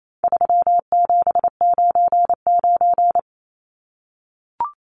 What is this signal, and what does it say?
Morse code, heard as a keyed tone, about 28 wpm, 700 Hz. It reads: 3799
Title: Numbers - 4-Digit 28wpm